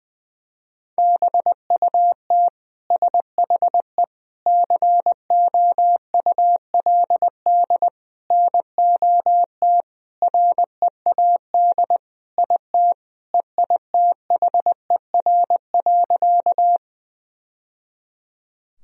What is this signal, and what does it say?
Morse code, heard as a keyed tone, about 21 wpm, 700 Hz. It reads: BUT SHE COULD NOT READ IT EITHER.
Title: Sentences from Top 1000 Words - Review Part 4 - 20wpm